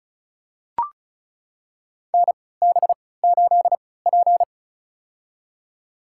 Morse code, heard as a keyed tone, about 36 wpm, spaced as for 20 wpm, 700 Hz. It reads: NB8P
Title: USA Call Signs - 4 Characters - 3x Character Spacing - 35wpm